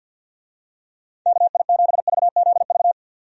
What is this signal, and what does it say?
Morse code, heard as a keyed tone, about 51 wpm, 700 Hz. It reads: KI7VZV
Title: USA Call Signs - 6 Characters - 50wpm